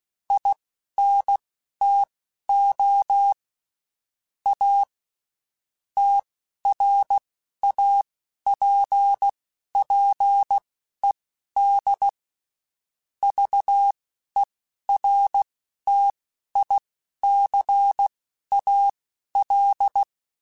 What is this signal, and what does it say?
Morse code, heard as a keyed tone, about 16 wpm, spaced as for 12 wpm, 800 Hz. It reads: INTO A TRAPPED VERTICAL